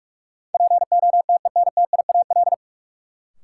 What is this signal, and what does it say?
Morse code, heard as a keyed tone, about 46 wpm, 700 Hz. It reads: POTENTIAL
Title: Sentences from Top 1000 Words - Review Part 4 - New Word Drill - 45wpm